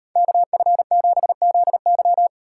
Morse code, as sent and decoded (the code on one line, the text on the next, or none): -.- ..-. --... --.. -.--
KF7ZY